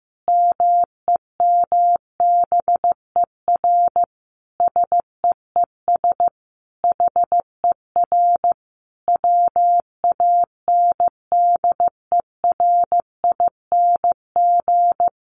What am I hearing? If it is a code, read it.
Morse code, heard as a keyed tone, about 15 wpm, 700 Hz. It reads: MEMBER SEES HER WANDERING
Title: Additional Easy Phrases 15wpm